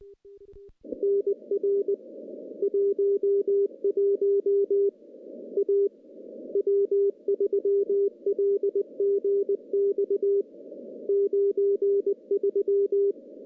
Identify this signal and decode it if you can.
Morse code, heard as a keyed tone, about 20 wpm, 400 Hz. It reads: NR 11 A W3LGX 93
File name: contests / CW-SS-k5tr- -0200